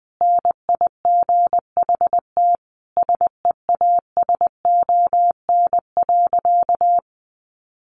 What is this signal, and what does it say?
Morse code, heard as a keyed tone, about 20 wpm, 700 Hz. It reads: NIGHT SEASON.
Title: Sentences from Top 1400 Words 20wpm